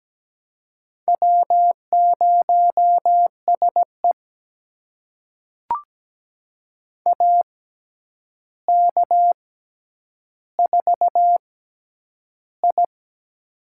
Morse code, heard as a keyed tone, about 17 wpm, spaced as for 9 wpm, 700 Hz. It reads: W0SE AK4I
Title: USA Call Signs - 4 Characters - 6x Character Spacing - 17wpm